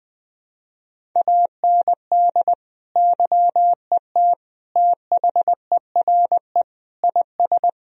Morse code, heard as a keyed tone, about 19 wpm, 700 Hz. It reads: AND YET THERE IS
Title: Sentences from Top 1000 Words - Review Part 1 - 20wpm